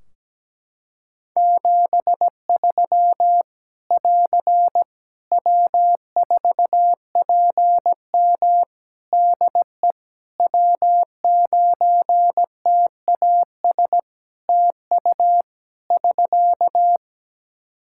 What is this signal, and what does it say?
Morse code, heard as a keyed tone, about 17 wpm, 700 Hz. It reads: 73 <AR> W4PM DE W9TAS TU <SK>
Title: K1USN SST QSO Practice 17wpm